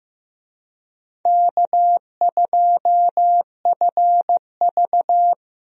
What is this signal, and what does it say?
Morse code, heard as a keyed tone, about 15 wpm, 700 Hz. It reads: K2FV
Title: USA Call Signs - 4 Characters - Repeated 3x - 15wpm